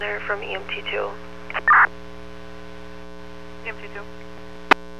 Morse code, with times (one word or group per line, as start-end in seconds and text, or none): none